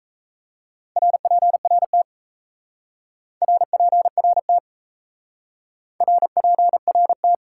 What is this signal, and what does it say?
Morse code, heard as a keyed tone, about 34 wpm, 700 Hz. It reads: RPRT RPRT RPRT
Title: Speed-Racing Top 90 QSO Elements 28wpm